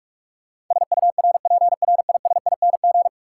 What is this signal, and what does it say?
Morse code, heard as a keyed tone, about 42 wpm, 700 Hz. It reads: SURPRISING
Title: Sets of 5 Words - New Word Drill - 45wpm